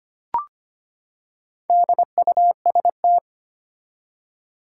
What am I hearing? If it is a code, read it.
Morse code, heard as a keyed tone, about 25 wpm, 700 Hz. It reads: DUST
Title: Four Letter Words - 25wpm